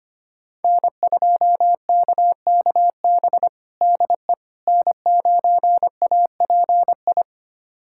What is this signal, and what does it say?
Morse code, heard as a keyed tone, about 25 wpm, 700 Hz. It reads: N2KKB DE N9API
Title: USA Callsign DE Callsign - 5 Characters - 25wpm